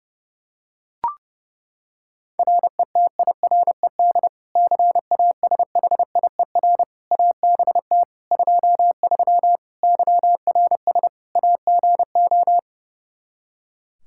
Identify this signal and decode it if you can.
Morse code, heard as a keyed tone, about 30 wpm, 700 Hz. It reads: RETIRED CASHIER ABT 23 YRS AGO
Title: Practice QSO - Exchange 3 - More Info - 30wpm